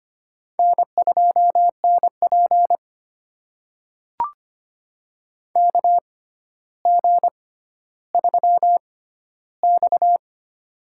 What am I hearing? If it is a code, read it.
Morse code, heard as a keyed tone, about 25 wpm, spaced as for 13 wpm, 700 Hz. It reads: N2NP KG3X
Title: USA Call Signs - 4 Characters - 6x Character Spacing - 25wpm